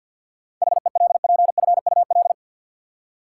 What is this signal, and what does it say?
Morse code, heard as a keyed tone, about 53 wpm, 700 Hz. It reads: HELPFUL
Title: Seven Letter Words - 50wpm